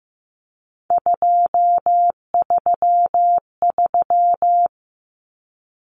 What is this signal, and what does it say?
Morse code, heard as a keyed tone, about 15 wpm, 700 Hz. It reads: 233
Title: Numbers - 3-Digit 15wpm